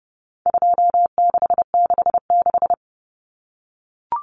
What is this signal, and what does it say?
Morse code, heard as a keyed tone, about 30 wpm, 700 Hz. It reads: 2666
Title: Common Cut Numbers - 4-Digit 30wpm